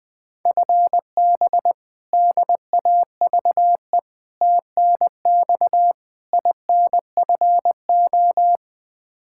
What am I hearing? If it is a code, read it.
Morse code, heard as a keyed tone, about 20 wpm, 700 Hz. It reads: FB DAVE TNX INFO
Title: K1USN SST QSO Practice 20wpm